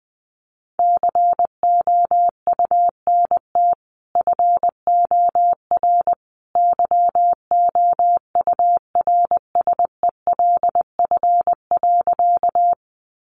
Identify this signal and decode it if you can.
Morse code, heard as a keyed tone, about 20 wpm, 700 Hz. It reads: COUNT FOR YOURSELF.
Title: Sentences from Top 1500 Words 20wpm